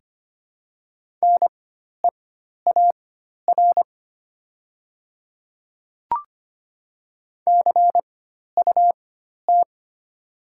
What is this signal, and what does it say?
Morse code, heard as a keyed tone, about 25 wpm, spaced as for 12 wpm, 700 Hz. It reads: NEAR CUT
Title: 4x Character Spacing - Top 300 Words - 25wpm